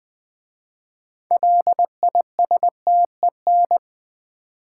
Morse code, heard as a keyed tone, about 20 wpm, 700 Hz. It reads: LISTEN
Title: 6 Letter Words - First 3 Letters Given - 20wpm